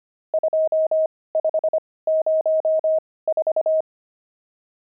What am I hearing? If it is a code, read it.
Morse code, heard as a keyed tone, about 25 wpm, spaced as for 18 wpm, 630 Hz. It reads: 2504
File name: pangram-fs-18-cs-25-nums15